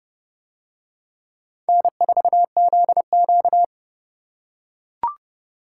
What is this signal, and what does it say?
Morse code, heard as a keyed tone, about 31 wpm, 700 Hz. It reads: N4ZQ